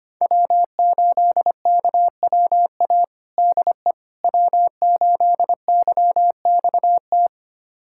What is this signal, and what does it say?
Morse code, heard as a keyed tone, about 25 wpm, 700 Hz. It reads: W8KWA DE W8YXT
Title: USA Callsign DE Callsign - 5 Characters - 25wpm